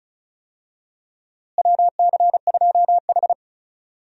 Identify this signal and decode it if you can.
Morse code, heard as a keyed tone, about 36 wpm, 700 Hz. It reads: WC2H
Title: USA Call Signs - 4 Characters - Repeated 2x - 35wpm